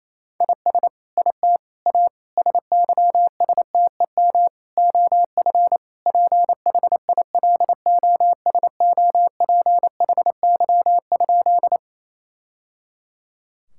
Morse code, heard as a keyed tone, about 28 wpm, 700 Hz. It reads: IS IT A SYSTEM OF PHILOSOPHY?
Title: Sentences from Top 1000 Words - Review Part 6 - 28wpm